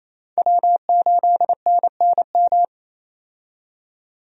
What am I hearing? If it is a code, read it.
Morse code, heard as a keyed tone, about 27 wpm, 700 Hz. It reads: W8NNM
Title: USA Call Signs - 5 Characters - 28wpm